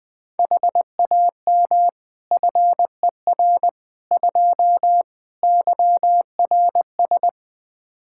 Morse code, heard as a keyed tone, about 21 wpm, 700 Hz. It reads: HAM FER 2 YRS
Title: Practice QSO - Exchange 3 - More Info - 20wpm